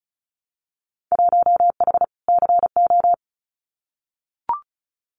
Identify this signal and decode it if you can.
Morse code, heard as a keyed tone, about 36 wpm, 700 Hz. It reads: 1H CO
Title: Winter Field Day Exchanges 35wpm